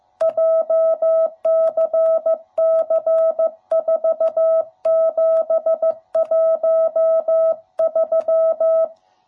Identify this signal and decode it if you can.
Morse code, heard as a keyed tone, about 14 wpm, 660 Hz. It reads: JCC4713
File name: jh8cbh / 2021cw / cw07b